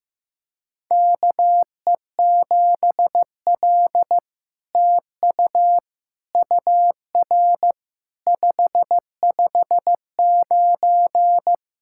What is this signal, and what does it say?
Morse code, heard as a keyed tone, about 15 wpm, 700 Hz. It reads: KE7L TU UR 559
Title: SOTA QSO Practice 15wpm